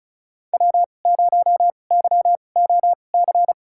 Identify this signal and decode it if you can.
Morse code, heard as a keyed tone, about 36 wpm, spaced as for 25 wpm, 700 Hz. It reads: W0YOC
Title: USA Call Signs - 5 Characters - 2x Character Spacing - 35wpm